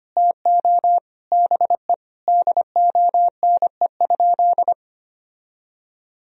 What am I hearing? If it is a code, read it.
Morse code, heard as a keyed tone, about 25 wpm, 700 Hz. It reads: TO BE DONE?
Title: Sentences from Top 1000 Words - Review Part 4 - 25wpm